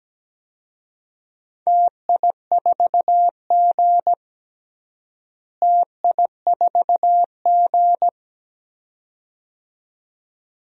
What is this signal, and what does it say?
Morse code, heard as a keyed tone, about 17 wpm, 700 Hz. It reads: TI4G TI4G
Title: DXCC-100 Call Signs - 4 Characters - Repeated 2x - 17wpm